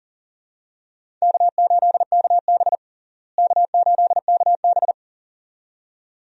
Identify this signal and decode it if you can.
Morse code, heard as a keyed tone, about 40 wpm, 700 Hz. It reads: K8KB K8KB
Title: USA Call Signs - 4 Characters - Repeated 2x - 40wpm